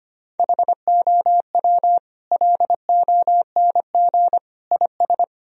Morse code, heard as a keyed tone, about 25 wpm, 700 Hz. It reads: HOW LONG IS
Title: Sentences from Top 1000 Words - Review Part 4 - 25wpm